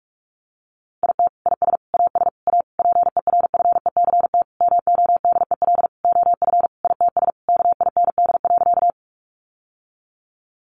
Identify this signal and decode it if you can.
Morse code, heard as a keyed tone, about 44 wpm, 700 Hz. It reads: IT IS AS A PERFECT MODEL OF ITS KIND.
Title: Sentences from Top 1000 Words 45wpm